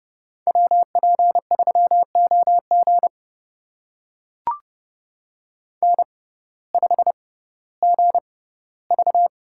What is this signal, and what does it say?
Morse code, heard as a keyed tone, about 30 wpm, spaced as for 17 wpm, 700 Hz. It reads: WP3OG N5GV
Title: USA Call Signs - 6x Character Spacing - 30wpm